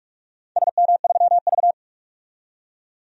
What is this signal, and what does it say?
Morse code, heard as a keyed tone, about 46 wpm, 690 Hz. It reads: SM3V
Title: DXCC-100 Call Signs - 4 Characters - 45wpm